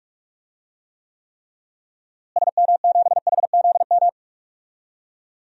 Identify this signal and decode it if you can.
Morse code, heard as a keyed tone, about 46 wpm, 700 Hz. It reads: SM7HZM
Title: DXCC-100 Call Signs - 6 Characters - Repeated 3x - 45wpm